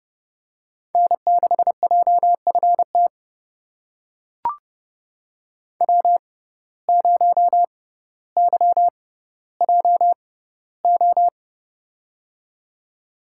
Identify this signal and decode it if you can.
Morse code, heard as a keyed tone, about 30 wpm, spaced as for 15 wpm, 700 Hz. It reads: N6JFT W0YJO
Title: USA Call Signs - 5 Characters - 6x Character Spacing - 30wpm